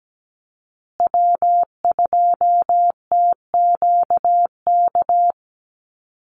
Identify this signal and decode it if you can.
Morse code, heard as a keyed tone, about 17 wpm, 700 Hz. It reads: W2TQK
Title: USA Call Signs Repeated 3x 17wpm